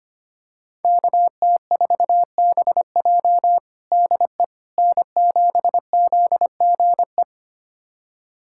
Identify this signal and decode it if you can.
Morse code, heard as a keyed tone, about 25 wpm, 700 Hz. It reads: KT4BJ DE N7ZGE
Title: USA Callsign DE Callsign - 5 Characters - 25wpm